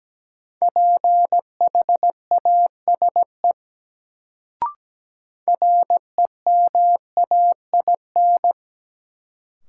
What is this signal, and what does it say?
Morse code, heard as a keyed tone, about 17 wpm, 700 Hz. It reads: PHASE REMAIN
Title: Sentences from Top 1200 Words - New Word Drill Alt - 17wpm